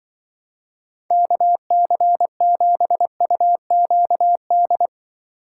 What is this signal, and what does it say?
Morse code, heard as a keyed tone, about 24 wpm, 700 Hz. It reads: KC7UQD
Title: Speed Racing - USA 2x3 Call Signs 20wpm